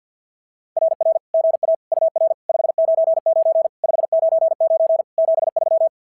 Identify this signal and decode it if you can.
Morse code, heard as a keyed tone, about 49 wpm, 650 Hz. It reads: RR GA UR 599 599 73
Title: SOTA QSO Practice 50wpm